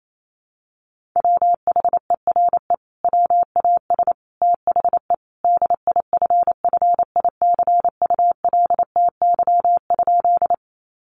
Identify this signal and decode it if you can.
Morse code, heard as a keyed tone, about 28 wpm, 700 Hz. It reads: WHERE WAS THE DIFFICULTY?
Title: Sentences from Top 1500 Words 28wpm